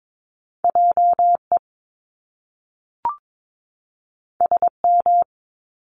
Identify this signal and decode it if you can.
Morse code, heard as a keyed tone, about 22 wpm, 700 Hz. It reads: JE SM
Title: Top Two Letter Combinations in Words 22wpm